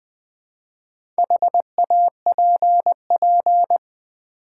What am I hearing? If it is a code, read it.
Morse code, heard as a keyed tone, about 20 wpm, 700 Hz. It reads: HAPP
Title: Build A Word - 2 Letters At A Time - 6-Letter Words - 20wpm